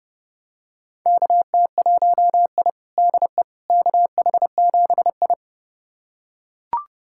Morse code, heard as a keyed tone, about 31 wpm, 700 Hz. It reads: KT1I DE KH7I
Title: USA Callsign DE Callsign - 4 Characters - 30wpm